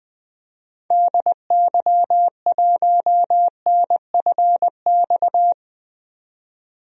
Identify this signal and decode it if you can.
Morse code, heard as a keyed tone, about 20 wpm, 700 Hz. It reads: DY1NFX
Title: DXCC-100 Call Signs - 6 Characters - 20wpm